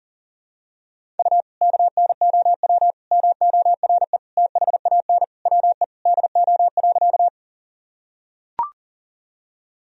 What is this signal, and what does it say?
Morse code, heard as a keyed tone, about 40 wpm, 700 Hz. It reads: U KNOW MORE THAN WE DO.
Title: Sentences from Top 100 Words and Abbreviations 40wpm